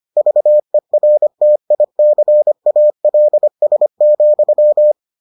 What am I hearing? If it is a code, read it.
Morse code, heard as a keyed tone, about 25 wpm, 600 Hz. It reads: VERTICALS,